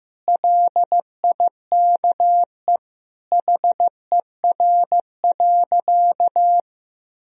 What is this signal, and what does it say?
Morse code, heard as a keyed tone, about 15 wpm, 700 Hz. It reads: LIKE HER.
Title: Sentences from Top 1000 Words - Review Part 2 - 15wpm